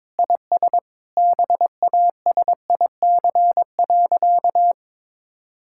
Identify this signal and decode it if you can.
Morse code, heard as a keyed tone, about 22 wpm, 700 Hz. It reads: IS BASIC.
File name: practice / 22wpm / Sentences with 3-Word Phrases - 22wpm